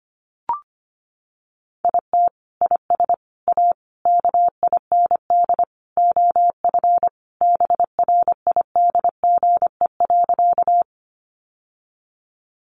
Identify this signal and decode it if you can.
Morse code, heard as a keyed tone, about 25 wpm, 700 Hz. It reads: IT IS A KIND OF BRIDGE.
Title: Sentences from Top 1200 Words 25wpm